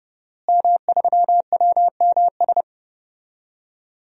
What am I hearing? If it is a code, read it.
Morse code, heard as a keyed tone, about 32 wpm, 700 Hz. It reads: M3WMS
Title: DXCC-100 Call Signs - 5 Characters - Repeated 3x - 30wpm